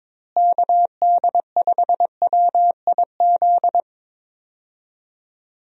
Morse code, heard as a keyed tone, about 22 wpm, 700 Hz. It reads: KD5WIZ
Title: USA Call Signs - 6 Characters - Repeated 2x - 22wpm